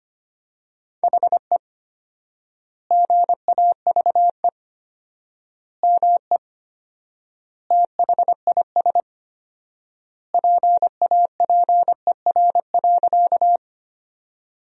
Morse code, heard as a keyed tone, about 25 wpm, 700 Hz. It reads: HE GAVE ME THIS PAPER.
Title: 4x Word-Spacing - Sentences from Top 500 Words - 25wpm